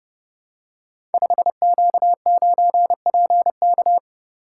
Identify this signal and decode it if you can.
Morse code, heard as a keyed tone, about 30 wpm, 700 Hz. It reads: 5Q9PK
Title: Code Groups - 5 Letter:Numbers - 30wpm